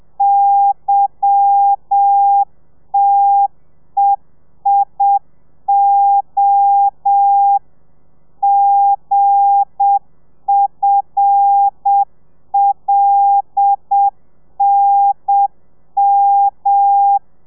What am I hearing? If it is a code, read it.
Morse code, heard as a keyed tone, about 7 wpm, 800 Hz. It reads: YTEIO GFLNM
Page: Lezione 1: audio a 30 CPM